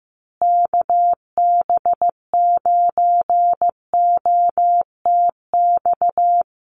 Morse code, heard as a keyed tone, about 15 wpm, 700 Hz. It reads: KB9OTX
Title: USA Call Signs Repeated 2x 15wpm